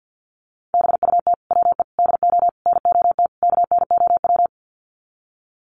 Morse code, heard as a keyed tone, about 49 wpm, 700 Hz. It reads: BUT WE DO NOT KNOW
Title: Sets of 5 Words Encore2 50wpm